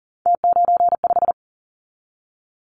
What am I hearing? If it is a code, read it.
Morse code, heard as a keyed tone, about 40 wpm, 700 Hz. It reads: T95
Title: Callsign Prefixes for Top 100 DXCC - 40wpm